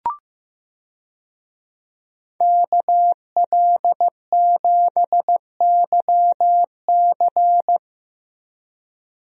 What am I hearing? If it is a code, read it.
Morse code, heard as a keyed tone, about 15 wpm, 700 Hz. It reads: KL7YC
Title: DXCC-100 Call Signs - 5 Characters - 15wpm